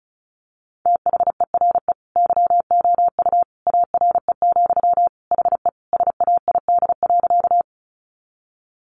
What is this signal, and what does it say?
Morse code, heard as a keyed tone, about 34 wpm, 700 Hz. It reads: THERE YOU ARE, HE SAID.
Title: Sentences from Top 100 Words 35wpm